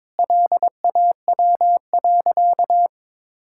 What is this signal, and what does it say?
Morse code, heard as a keyed tone, about 22 wpm, 700 Hz. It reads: LAW.
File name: practice / 22wpm / Sentences with 8-Word Phrases - 22wpm